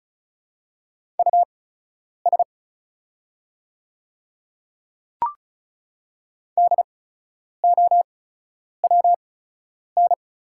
Morse code, heard as a keyed tone, about 36 wpm, spaced as for 10 wpm, 690 Hz. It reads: US DOWN
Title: 8x Character Spacing - Top 200 Words - 35wpm